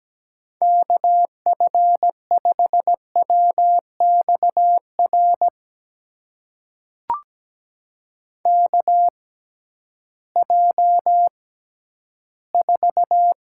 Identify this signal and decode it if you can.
Morse code, heard as a keyed tone, about 17 wpm, spaced as for 11 wpm, 700 Hz. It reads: KF5WXR KJ4
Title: USA Call Signs - 6 Characters - 6x Character Spacing - 17wpm